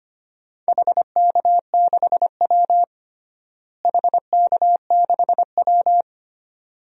Morse code, heard as a keyed tone, about 25 wpm, 700 Hz. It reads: HK6W HK6W
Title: DXCC-100 Call Signs - 4 Characters - Repeated 2x - 25wpm